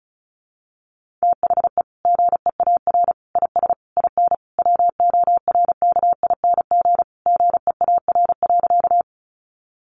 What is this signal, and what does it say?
Morse code, heard as a keyed tone, about 34 wpm, 700 Hz. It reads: THE GEAR IS IN WORKING GEAR.
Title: Sentences from Top 1400 Words 35wpm